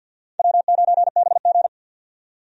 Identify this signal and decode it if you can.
Morse code, heard as a keyed tone, about 51 wpm, 700 Hz. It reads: W9BG
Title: USA Call Signs - 4 Characters - 50wpm